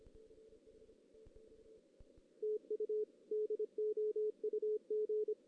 Morse code, heard as a keyed tone, about 23 wpm, 430 Hz. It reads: TU DOUG
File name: 2016 JAN CW NAQP / 2016 Jan naqp k5ot - 0-3